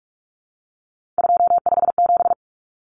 Morse code, heard as a keyed tone, about 44 wpm, 700 Hz. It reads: 257
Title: Common Cut Numbers - 3-Digit 45wpm